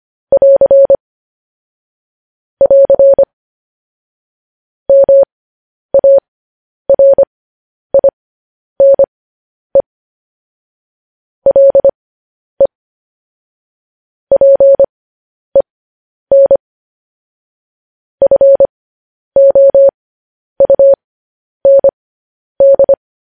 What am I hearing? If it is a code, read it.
Morse code, heard as a keyed tone, about 25 wpm, spaced as for 10 wpm, 550 Hz. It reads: <AR> <AR> MARINE LE PEN FOUND
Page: news-headlines-in-morse-code-at-10-wpm